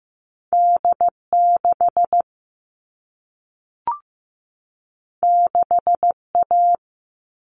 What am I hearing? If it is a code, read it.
Morse code, heard as a keyed tone, about 15 wpm, 700 Hz. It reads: D6 6A
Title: All Cut Numbers - 2-Digit 15wpm